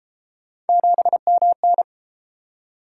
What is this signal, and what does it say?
Morse code, heard as a keyed tone, about 33 wpm, 700 Hz. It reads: 7MN